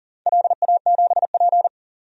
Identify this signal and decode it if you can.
Morse code, heard as a keyed tone, about 38 wpm, 700 Hz. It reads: LA7P
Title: DXCC-100 Call Signs - 4 Characters - 40wpm